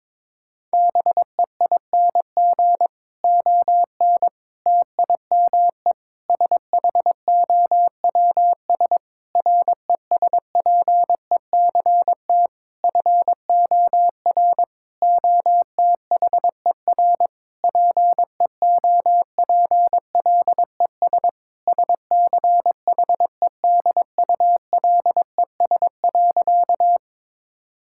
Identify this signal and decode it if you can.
Morse code, heard as a keyed tone, about 23 wpm, 700 Hz. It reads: BEING ON TIME SHOWS RESPECT FOR OTHER PEOPLES SCHEDULES.
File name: Difficult Sentences with Hints - 22wpm